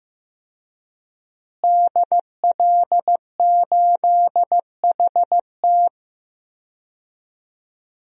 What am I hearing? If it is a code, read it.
Morse code, heard as a keyed tone, about 15 wpm, 700 Hz. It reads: DL8HT
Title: DXCC-100 Call Signs - 5 Characters - Repeated 3x - 15wpm